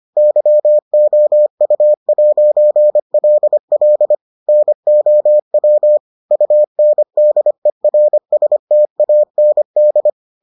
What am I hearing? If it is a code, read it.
Morse code, heard as a keyed tone, about 25 wpm, 600 Hz. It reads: YOU'LL NOW UNDERSTAND